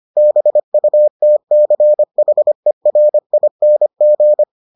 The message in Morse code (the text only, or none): BUTCHERING